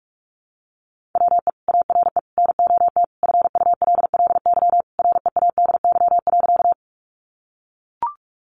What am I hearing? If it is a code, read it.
Morse code, heard as a keyed tone, about 44 wpm, 700 Hz. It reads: WE ARE NOT FULLY READY.
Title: Sentences from Top 1300 Words 45wpm